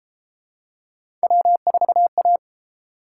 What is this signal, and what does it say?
Morse code, heard as a keyed tone, about 30 wpm, 700 Hz. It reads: W4A